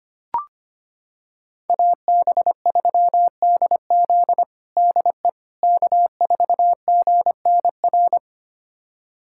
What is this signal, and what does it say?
Morse code, heard as a keyed tone, about 25 wpm, 700 Hz. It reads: AB3DZ DE K4GNR
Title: USA Callsign DE Callsign - 5 Characters - 25wpm